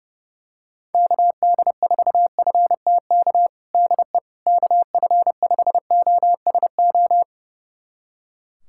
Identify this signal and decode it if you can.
Morse code, heard as a keyed tone, about 30 wpm, 700 Hz. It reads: KD4FTK DE KF5OSO
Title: USA Callsign DE Callsign 30wpm